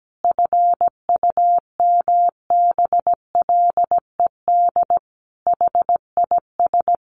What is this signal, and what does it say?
Morse code, heard as a keyed tone, about 17 wpm, 700 Hz. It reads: FUMBLED HIS
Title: A Sound Of Thunder 17wpm